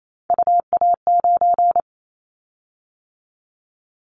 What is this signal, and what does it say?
Morse code, heard as a keyed tone, about 28 wpm, 700 Hz. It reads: UA9
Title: Callsign Prefixes for Top 100 DXCC - 28wpm